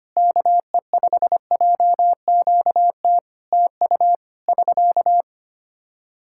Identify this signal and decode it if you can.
Morse code, heard as a keyed tone, about 25 wpm, 700 Hz. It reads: KE5JQT TU <SK>
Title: Practice QSO - Exchange 4 - Ending - 25wpm